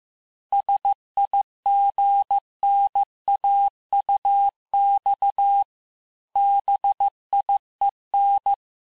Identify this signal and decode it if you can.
Morse code, heard as a keyed tone, about 15 wpm, 800 Hz. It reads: SIGNAUX BIEN
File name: cours-du-lundi files / cours-1585-texte-FIN-de-QSO-vitesse-14mpm